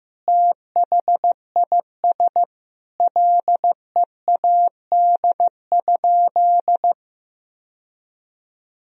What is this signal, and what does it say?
Morse code, heard as a keyed tone, about 16 wpm, 700 Hz. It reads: THIS LEAD?
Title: Sentences from Top 1000 Words - Review Part 2 - 15wpm